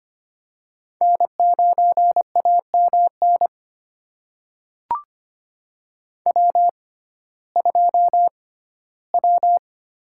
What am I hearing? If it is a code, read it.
Morse code, heard as a keyed tone, about 25 wpm, spaced as for 15 wpm, 700 Hz. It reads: N9AMN W2W
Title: USA Call Signs - 6x Character Spacing - 25wpm